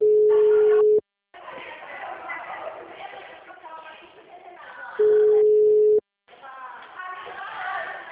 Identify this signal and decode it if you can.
Ringback tone (Europe, 425 Hz), with speech.